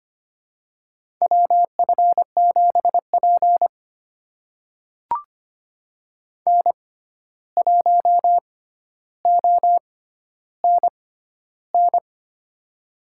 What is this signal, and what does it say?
Morse code, heard as a keyed tone, about 25 wpm, spaced as for 12 wpm, 700 Hz. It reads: WF7P N1ONN
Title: USA Call Signs - 6x Character Spacing - 25wpm